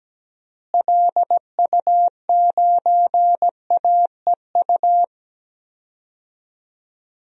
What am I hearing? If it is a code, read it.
Morse code, heard as a keyed tone, about 17 wpm, 700 Hz. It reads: LU9AEU
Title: DXCC-100 Call Signs - 6 Characters - Repeated 2x - 17wpm